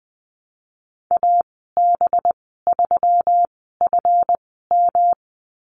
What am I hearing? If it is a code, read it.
Morse code, heard as a keyed tone, about 19 wpm, spaced as for 14 wpm, 700 Hz. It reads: AB3FM
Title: 193 Lesson 31g - Letter v - Call Signs 20wpm